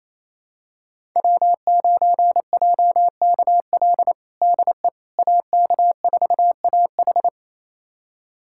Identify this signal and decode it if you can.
Morse code, heard as a keyed tone, about 28 wpm, 700 Hz. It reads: W9JKL DE AK4AH
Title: USA Callsign DE Callsign 28wpm